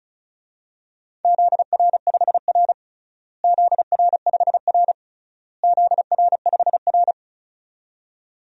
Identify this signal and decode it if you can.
Morse code, heard as a keyed tone, about 34 wpm, 700 Hz. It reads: ZR5R ZR5R ZR5R
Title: DXCC-100 Call Signs - 4 Characters - Repeated 3x - 35wpm